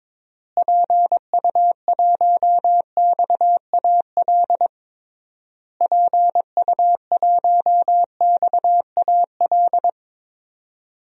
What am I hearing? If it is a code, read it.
Morse code, heard as a keyed tone, about 21 wpm, 700 Hz. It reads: PU1XAL PU1XAL
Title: DXCC-100 Call Signs - 6 Characters - Repeated 3x - 22wpm